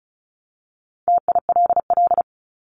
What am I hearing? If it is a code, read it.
Morse code, heard as a keyed tone, about 34 wpm, 700 Hz. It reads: TILL
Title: Top 1751-2000 Words 35wpm